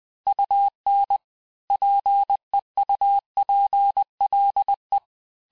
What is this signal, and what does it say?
Morse code, heard as a keyed tone, about 20 wpm, 800 Hz. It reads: UN PEUPLE
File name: cours-du-mercredi / audios / cours-858-texte-clais-vitesse-19mpm